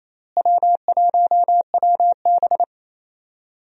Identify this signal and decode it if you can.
Morse code, heard as a keyed tone, about 28 wpm, 700 Hz. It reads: W1WB
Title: USA Call Signs - 4 Characters - Repeated 3x - 28wpm